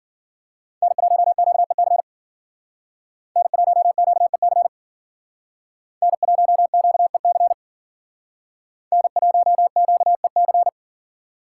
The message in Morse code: N1QEC N1QEC N1QEC N1QEC

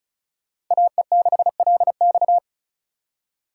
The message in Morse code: AE6LX

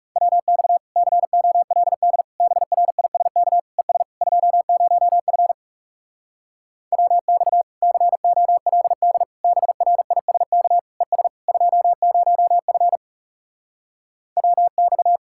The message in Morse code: WX COLD BRISK ES 20F WX COLD BRISK ES 20F WX

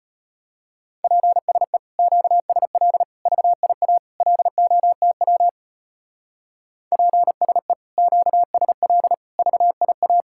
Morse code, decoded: PSE QSL VIA LOTW PSE QSL VIA

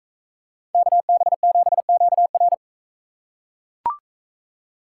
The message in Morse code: KB7QR

